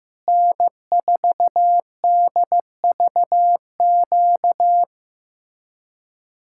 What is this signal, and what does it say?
Morse code, heard as a keyed tone, about 15 wpm, 700 Hz. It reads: N4DVQ
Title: Speed-Racing Common QSO Phrases Encore - 15wpm